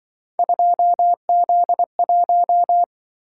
Morse code, heard as a keyed tone, about 25 wpm, 700 Hz. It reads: 2Z1
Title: Speed-Racing - Three Letter-Number Combinations - 20wpm